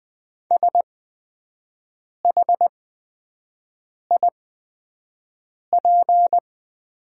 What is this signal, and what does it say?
Morse code, heard as a keyed tone, about 20 wpm, spaced as for 5 wpm, 700 Hz. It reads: SHIP
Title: 8x Character Spacing - Top 500 Words - 20wpm